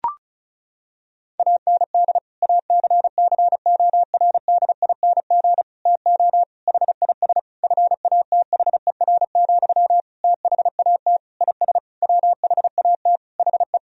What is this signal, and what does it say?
Morse code, heard as a keyed tone, about 34 wpm, 700 Hz. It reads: AND ACCORDING TO HIS FATHER, THAT IS WHAT HE
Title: Sentences from Top 1000 Words - Review Part 4 - 35wpm